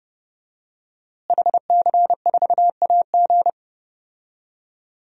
Morse code, heard as a keyed tone, about 32 wpm, 700 Hz. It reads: HC4AG
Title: DXCC-100 Call Signs - 5 Characters - Repeated 2x - 30wpm